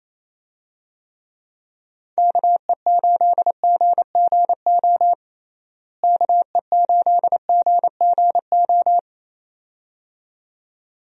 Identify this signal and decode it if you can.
Morse code, heard as a keyed tone, about 27 wpm, 700 Hz. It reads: KE8GGO KE8GGO
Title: USA Call Signs - 6 Characters - Repeated 2x - 28wpm